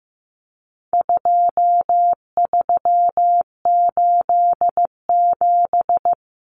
Morse code, heard as a keyed tone, about 15 wpm, 700 Hz. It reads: 2387
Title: Common Cut Numbers - 4-Digit 15wpm